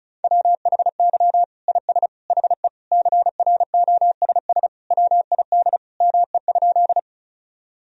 Morse code, heard as a keyed tone, about 34 wpm, 700 Hz. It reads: WHY IS HE CROSS WID ME?
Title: Sentences from Top 300 Words and Abbreviations 35wpm